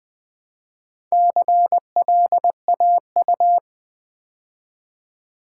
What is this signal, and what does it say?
Morse code, heard as a keyed tone, about 20 wpm, 700 Hz. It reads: CLAU
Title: Build A Word - 2 Letters At A Time - 6-Letter Words - 20wpm